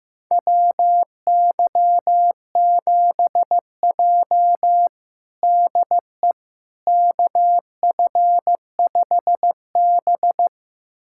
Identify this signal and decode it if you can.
Morse code, heard as a keyed tone, about 15 wpm, 700 Hz. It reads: WY7J DE KF5B
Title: USA Callsign DE Callsign - 4 Characters - 15wpm